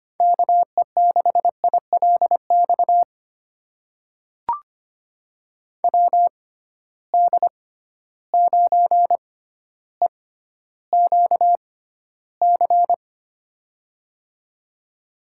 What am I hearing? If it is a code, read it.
Morse code, heard as a keyed tone, about 25 wpm, spaced as for 13 wpm, 700 Hz. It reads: KE6ILX WD9EQC
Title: USA Call Signs - 6 Characters - 6x Character Spacing - 25wpm